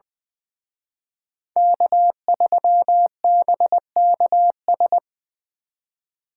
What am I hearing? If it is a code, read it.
Morse code, heard as a keyed tone, about 24 wpm, 700 Hz. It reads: K3BKS
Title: Speed Racing - USA 1x3 Call Signs 20wpm